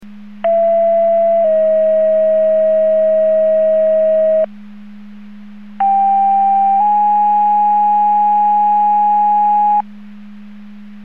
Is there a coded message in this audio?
Yes: two-tone page from 0.4 to 9.8 s, heard as a one-second tone, then a three-second tone, about 670 then 640 Hz, about 790 then 830 Hz.